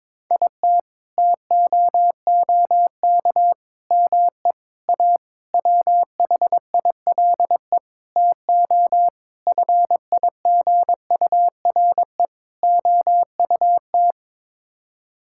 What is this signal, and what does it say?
Morse code, heard as a keyed tone, about 22 wpm, 700 Hz. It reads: IT TOOK ME A WHILE TO FIGURE OUT
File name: practice / 22wpm / Sets of 8 Words - 22wpm